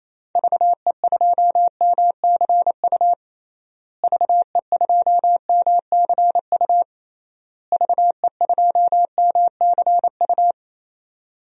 Morse code, heard as a keyed tone, about 28 wpm, 700 Hz. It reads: VE2MCU VE2MCU VE2MCU
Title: DXCC-100 Call Signs - 6 Characters - Repeated 3x - 28wpm